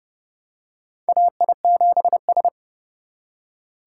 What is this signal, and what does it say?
Morse code, heard as a keyed tone, about 30 wpm, 700 Hz. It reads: AI7S
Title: USA Call Signs - 4 Characters - 3x Character Spacing - 30wpm